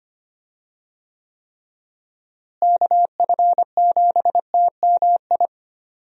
Morse code, heard as a keyed tone, about 25 wpm, 700 Hz. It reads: KF7TMI
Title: USA Call Signs - 6 Characters - Repeated 3x - 25wpm